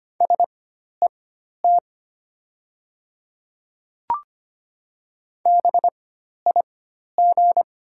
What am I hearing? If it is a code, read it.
Morse code, heard as a keyed tone, about 25 wpm, spaced as for 12 wpm, 700 Hz. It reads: SET BIG
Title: 4x Character Spacing - Top 200 Words - 25wpm